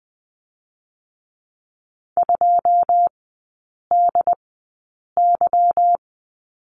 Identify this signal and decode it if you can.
Morse code, heard as a keyed tone, about 20 wpm, spaced as for 8 wpm, 700 Hz. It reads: 2DY
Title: 145 Lesson 24e - Letter y - Three Characters 20wpm